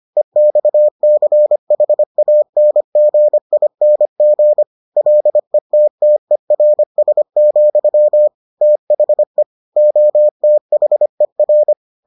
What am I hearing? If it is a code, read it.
Morse code, heard as a keyed tone, about 25 wpm, 600 Hz. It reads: EXCHANGING LETTERS, THE OTHER